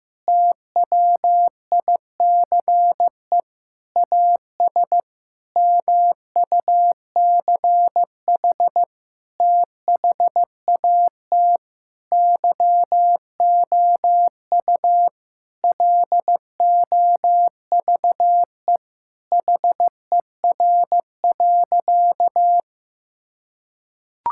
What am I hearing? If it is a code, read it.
Morse code, heard as a keyed tone, about 15 wpm, 700 Hz. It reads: TWICE AS MUCH THAT YOU LOVE HER.
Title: Sentences from Top 1600 Words 15wpm